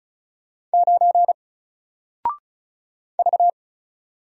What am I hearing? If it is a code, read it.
Morse code, heard as a keyed tone, about 36 wpm, 700 Hz. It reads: 9 V
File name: Single-Letter-Number-35wpm